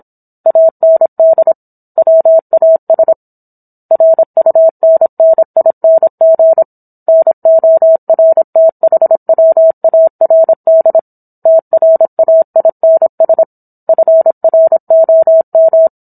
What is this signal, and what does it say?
Morse code, heard as a keyed tone, about 27 wpm, 650 Hz. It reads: AND WAS RUNNING NORTHWARD TRAINS FROM